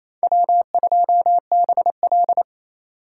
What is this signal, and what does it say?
Morse code, heard as a keyed tone, about 32 wpm, 700 Hz. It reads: W2BL
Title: Speed Racing - USA 1x2 Call Signs 28wpm